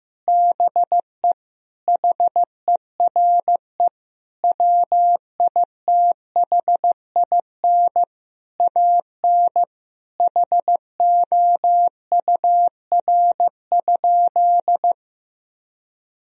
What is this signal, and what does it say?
Morse code, heard as a keyed tone, about 15 wpm, 700 Hz. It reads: BE HERE WITHIN AN HOUR?
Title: Sentences from Top 1000 Words - Review Part 4 - 15wpm